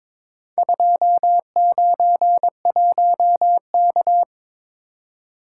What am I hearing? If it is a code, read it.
Morse code, heard as a keyed tone, about 22 wpm, 700 Hz. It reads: 291K
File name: practice / 22wpm / Four Letter-Number Combinations - 22wpm